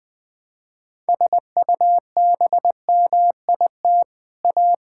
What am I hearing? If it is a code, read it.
Morse code, heard as a keyed tone, about 20 wpm, 700 Hz. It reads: SUBMIT A
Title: Sets of 6 Words - 20wpm